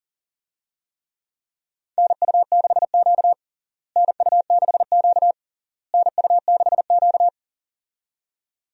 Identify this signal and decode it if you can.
Morse code, heard as a keyed tone, about 40 wpm, 700 Hz. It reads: NU6Q NU6Q NU6Q
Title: USA Call Signs - 4 Characters - Repeated 3x - 40wpm